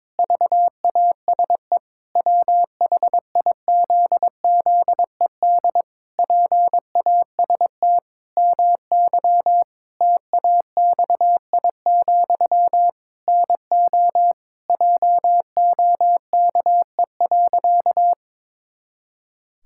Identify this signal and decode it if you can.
Morse code, heard as a keyed tone, about 22 wpm, 700 Hz. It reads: VASE WHIZZED PAST MY TAXI, NO JOKE.
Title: Pangram Challenge v2 - 22wpm